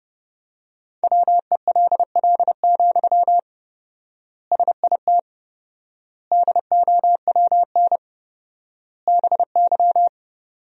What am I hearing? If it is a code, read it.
Morse code, heard as a keyed tone, about 30 wpm, 700 Hz. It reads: WELL, SIT DOWN BY
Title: 4x Word-Spacing - Sentences from Top 500 Words - 30wpm